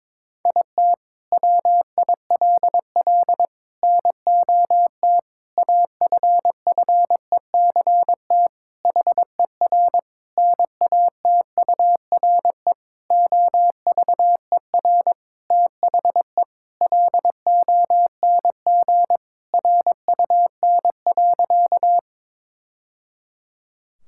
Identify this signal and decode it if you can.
Morse code, heard as a keyed tone, about 21 wpm, 700 Hz. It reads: IT WILL NOT AFFECT HER NATURE OVER THE LONG RUN.
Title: Sentences from Top 1000 Words - Review Part 6 - 22wpm